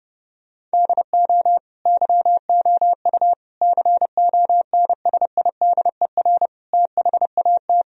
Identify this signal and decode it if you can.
Morse code, heard as a keyed tone, about 30 wpm, 700 Hz. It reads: DO YOU CONSIDER THAT
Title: Sentences from Top 1000 Words - Review Part 4 - 30wpm